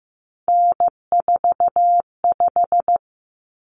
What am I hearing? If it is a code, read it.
Morse code, heard as a keyed tone, about 15 wpm, 700 Hz. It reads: N45
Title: Common Cut Numbers - 3-Digit 15wpm